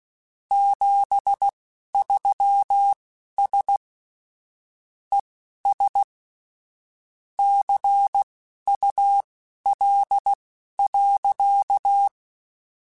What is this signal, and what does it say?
Morse code, heard as a keyed tone, about 16 wpm, spaced as for 11 wpm, 800 Hz. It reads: 73S ES CUL.